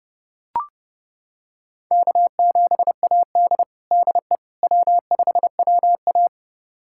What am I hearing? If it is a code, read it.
Morse code, heard as a keyed tone, about 30 wpm, 700 Hz. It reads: K7AD DE W5WA
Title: USA Callsign DE Callsign - 4 Characters - 30wpm